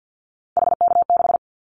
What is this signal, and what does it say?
Morse code, heard as a keyed tone, about 50 wpm, 700 Hz. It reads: HK6
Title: Callsign Prefixes for Top 100 DXCC - 50wpm